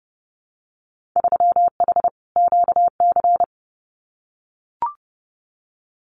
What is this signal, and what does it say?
Morse code, heard as a keyed tone, about 30 wpm, 700 Hz. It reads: 3H QC
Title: Winter Field Day Exchanges 30wpm